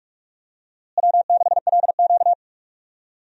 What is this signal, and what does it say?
Morse code, heard as a keyed tone, about 46 wpm, 700 Hz. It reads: W6LQ
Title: USA Call Signs - 4 Characters - Repeated 3x - 45wpm